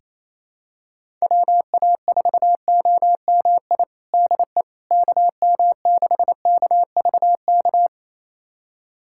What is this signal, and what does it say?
Morse code, heard as a keyed tone, about 27 wpm, 700 Hz. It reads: WA4OMI DE KM6KVK
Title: USA Callsign DE Callsign - 6 Characters - 28wpm